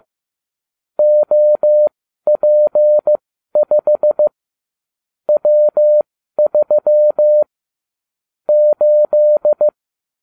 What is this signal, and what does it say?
Morse code, heard as a keyed tone, about 15 wpm, spaced as for 12 wpm, 600 Hz. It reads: OP5 W3 8
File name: cw / ubacw29 2